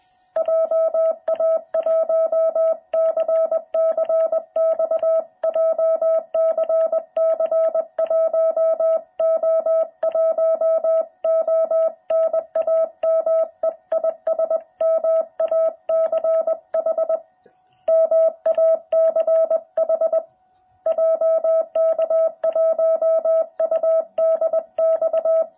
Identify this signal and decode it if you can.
Morse code, heard as a keyed tone, about 20 wpm, 660 Hz. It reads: JA1CCX JCC 1O1O NAME IS MACH MACH JK1UDX